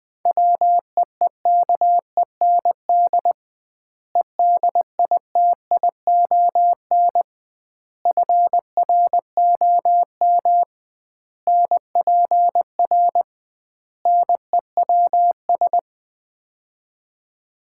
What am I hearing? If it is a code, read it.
Morse code, heard as a keyed tone, about 20 wpm, 700 Hz. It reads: WEEKEND EDITION FROM NPR NEWS
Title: 2x Word-Spacing - Sets of 5 Words - 20wpm